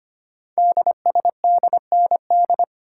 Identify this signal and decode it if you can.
Morse code, heard as a keyed tone, about 25 wpm, 700 Hz. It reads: DSDND
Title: Code Groups - 5 Letters - 25wpm